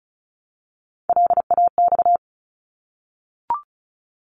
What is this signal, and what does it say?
Morse code, heard as a keyed tone, about 36 wpm, 700 Hz. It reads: LAX
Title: ARRL Sections 35wpm